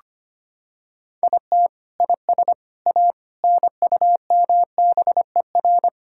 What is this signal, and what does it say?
Morse code, heard as a keyed tone, about 25 wpm, 700 Hz. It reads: IT IS A NUMBER
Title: Sentences from Top 1000 Words - Review Part 5 - 25wpm